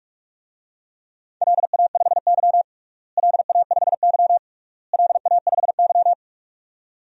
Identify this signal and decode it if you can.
Morse code, heard as a keyed tone, about 46 wpm, 690 Hz. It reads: LA5Y LA5Y LA5Y
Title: DXCC-100 Call Signs - 4 Characters - Repeated 3x - 45wpm